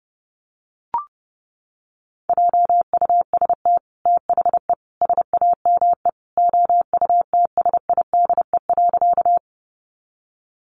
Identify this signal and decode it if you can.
Morse code, heard as a keyed tone, about 30 wpm, 700 Hz. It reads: JUST THE SAME OUTSIDE.
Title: Sentences from Top 900 Words 30wpm